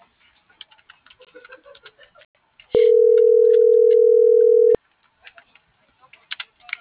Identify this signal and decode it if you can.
Ringback tone (North America), 440 + 480 Hz.